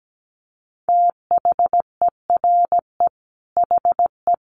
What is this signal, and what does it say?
Morse code, heard as a keyed tone, about 17 wpm, 700 Hz. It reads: THERE HE
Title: Sentences from Top 100 Words 17wpm